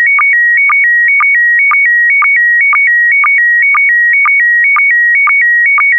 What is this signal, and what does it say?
SSTV (slow-scan television), PD 120.
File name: ISS-SSTV / SSTV-Audio-ISS-Bild-10